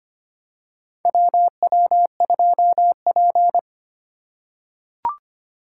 Morse code, heard as a keyed tone, about 25 wpm, 700 Hz. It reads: WW2P
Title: K1USN SST QSO Practice - 25wpm